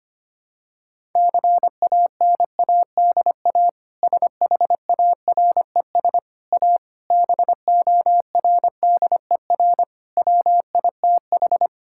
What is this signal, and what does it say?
Morse code, heard as a keyed tone, about 25 wpm, 700 Hz. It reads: CANADA SHARES A BORDER WITH
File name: practice / 25wpm / Sentences with 4-Word Phrases - 25wpm